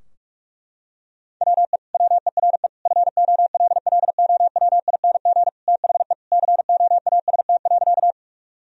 Morse code, heard as a keyed tone, about 47 wpm, 700 Hz. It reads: WE WERE FOLLOWING THE COAST.
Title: Sentences from Top 1000 Words - Review Part 4 - 45wpm